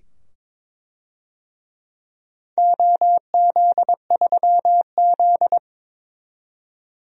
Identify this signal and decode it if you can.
Morse code, heard as a keyed tone, about 22 wpm, 700 Hz. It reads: OZ3Z
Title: DXCC-100 Call Signs - 4 Characters - 22wpm